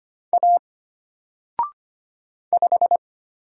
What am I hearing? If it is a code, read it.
Morse code, heard as a keyed tone, about 25 wpm, 700 Hz. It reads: A 5
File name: Single-Letter-Number-25wpm